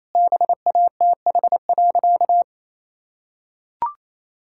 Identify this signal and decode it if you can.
Morse code, heard as a keyed tone, about 28 wpm, 700 Hz. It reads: BATH.
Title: Sentences from Top 1600 Words 28wpm